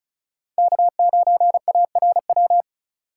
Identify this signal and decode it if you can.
Morse code, heard as a keyed tone, about 36 wpm, 700 Hz. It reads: K9ARW
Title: USA Call Signs - 5 Characters - 2x Character Spacing - 35wpm